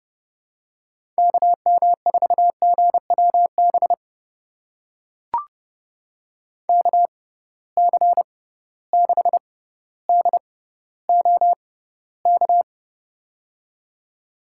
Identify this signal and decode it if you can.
Morse code, heard as a keyed tone, about 32 wpm, spaced as for 16 wpm, 700 Hz. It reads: KM4GWB KC6DOK
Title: USA Call Signs - 6 Characters - 6x Character Spacing - 30wpm